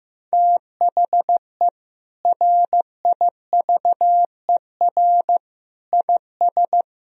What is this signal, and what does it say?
Morse code, heard as a keyed tone, about 15 wpm, 700 Hz. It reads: THE RIVER IS
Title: Sentences from Top 1000 Words - Review Part 1 - 15wpm